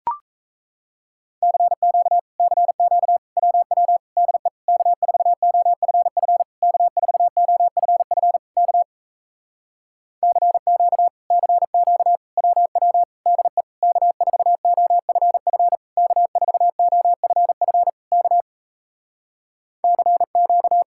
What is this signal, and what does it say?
Morse code, heard as a keyed tone, about 37 wpm, 700 Hz. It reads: CQ CQ WW DE K4OFF K4OFF K CQ CQ WW DE K4OFF K4OFF K CQ